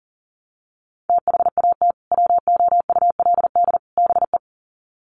Morse code, heard as a keyed tone, about 37 wpm, 700 Hz. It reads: THAT WOULD BE
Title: Sets of 3 Words 40wpm